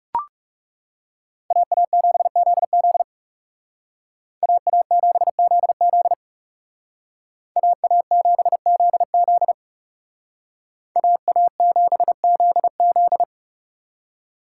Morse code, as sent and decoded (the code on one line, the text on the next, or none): .- .- --... --.. --.. / .- .- --... --.. --.. / .- .- --... --.. --.. / .- .- --... --.. --..
AA7ZZ AA7ZZ AA7ZZ AA7ZZ